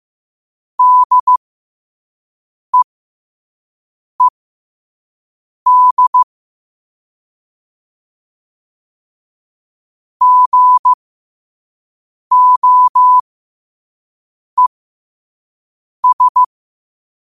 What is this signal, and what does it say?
Morse code, heard as a keyed tone, about 15 wpm, spaced as for 5 wpm, 1000 Hz. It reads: DEED GOES
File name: qotdfiles / 5wpm